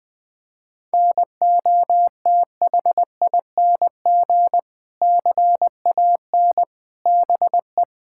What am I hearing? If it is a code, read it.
Morse code, heard as a keyed tone, about 19 wpm, 700 Hz. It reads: NOTHING CAN BE
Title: Sentences from Top 1000 Words - Review Part 6 - 20wpm